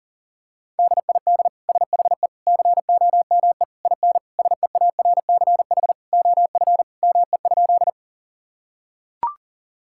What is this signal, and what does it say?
Morse code, heard as a keyed tone, about 40 wpm, 700 Hz. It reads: DID SHE COME IN SEARCH OF ME?
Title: Sentences from Top 1000 Words - Review Part 5 - 40wpm